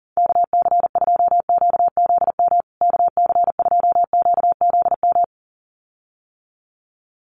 Morse code, heard as a keyed tone, about 40 wpm, 700 Hz. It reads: KC2QZM KC2QZM
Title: USA Call Signs Repeated 2x 40wpm